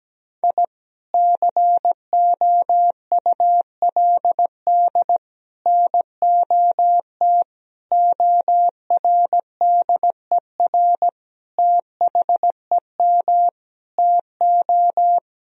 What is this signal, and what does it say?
Morse code, heard as a keyed tone, about 17 wpm, 700 Hz. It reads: I COULD NOT ORDER THEM TO
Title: Sentences from Top 1000 Words - Review Part 2 - 17wpm